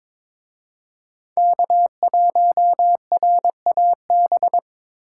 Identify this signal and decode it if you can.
Morse code, heard as a keyed tone, about 25 wpm, 700 Hz. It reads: K1RAB
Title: Speed Racing - USA 1x3 Call Signs 22wpm